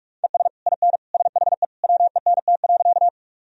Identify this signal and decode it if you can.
Morse code, heard as a keyed tone, about 43 wpm, 700 Hz. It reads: ES IN SHE WENT.
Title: Sentences from Top 200 Words and Abbreviations 45wpm